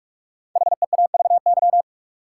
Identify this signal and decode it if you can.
Morse code, heard as a keyed tone, about 44 wpm, 700 Hz. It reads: HEAVY
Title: Sentences from Top 1100 Words - New Word Drill Alt - 45wpm